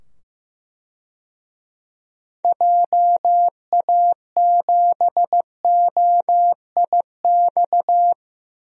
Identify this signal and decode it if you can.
Morse code, heard as a keyed tone, about 15 wpm, 700 Hz. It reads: JA7OIX
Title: DXCC-100 Call Signs - 6 Characters - 15wpm